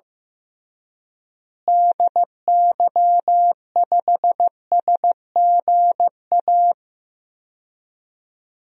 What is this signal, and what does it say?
Morse code, heard as a keyed tone, about 15 wpm, 700 Hz. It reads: DY5SGA
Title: DXCC-100 Call Signs - 6 Characters - Repeated 2x - 15wpm